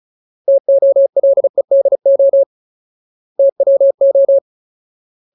Morse code, heard as a keyed tone, about 34 wpm, 550 Hz. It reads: TOLEDO TWO
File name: Ultimate Word List / 35WPM Ultimate